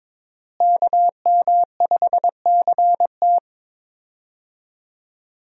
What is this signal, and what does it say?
Morse code, heard as a keyed tone, about 21 wpm, 700 Hz. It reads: KM5CT
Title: USA Call Signs - 5 Characters - Repeated 2x - 22wpm